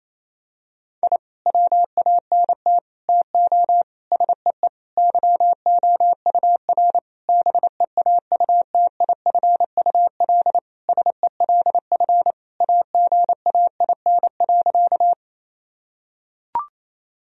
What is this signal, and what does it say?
Morse code, heard as a keyed tone, about 28 wpm, 700 Hz. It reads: I WANT TO SEE YOUR BEAUTIFUL SELF AGAIN.
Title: Sentences from Top 1000 Words - Review Part 1 - 28wpm